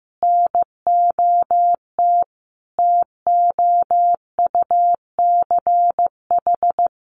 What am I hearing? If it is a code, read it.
Morse code, heard as a keyed tone, about 15 wpm, 700 Hz. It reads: NOT TOUCH
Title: Sentences from Top 1200 Words 15wpm